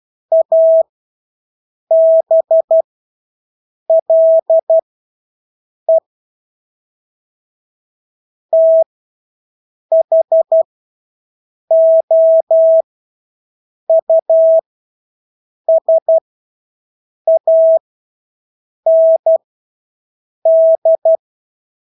Morse code, heard as a keyed tone, about 12 wpm, spaced as for 6 wpm, 650 Hz. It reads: ABLE THOUSAND
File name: Engelske ord 12-6 wpm